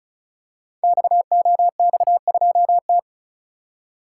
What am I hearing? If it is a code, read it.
Morse code, heard as a keyed tone, about 36 wpm, 690 Hz. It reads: XOX2T
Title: Code Groups - 5 Letter:Numbers - 35wpm